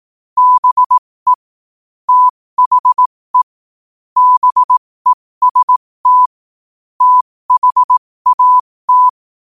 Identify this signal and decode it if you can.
Morse code, heard as a keyed tone, about 18 wpm, spaced as for 15 wpm, 1000 Hz. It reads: BE THE BEST THAT
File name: qotdfiles / 15wpm